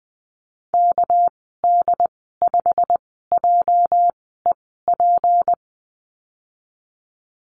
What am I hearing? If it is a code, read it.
Morse code, heard as a keyed tone, about 20 wpm, spaced as for 14 wpm, 700 Hz. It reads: KD5JEP
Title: 213 Lesson 34g - Number 8 - Call Signs 20wpm